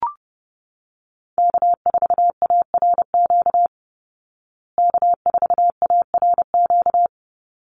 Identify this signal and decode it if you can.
Morse code, heard as a keyed tone, about 30 wpm, 700 Hz. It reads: K4ARQ K4ARQ
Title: Call Signs Repeated 3x - 30wpm